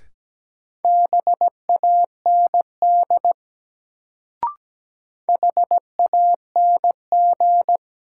Morse code, heard as a keyed tone, about 17 wpm, 700 Hz. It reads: BAND HANG
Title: Sentences from Top 1600 Words - New Word Drill Alt - 17wpm